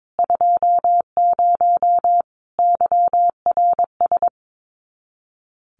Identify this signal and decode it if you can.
Morse code, heard as a keyed tone, about 22 wpm, 700 Hz. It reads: 20 YRS
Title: Common QSO Phrases 22wpm